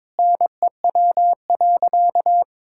Morse code, heard as a keyed tone, about 22 wpm, 700 Hz. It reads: NEW.
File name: Sentences with 9-Word Phrases - 22wpm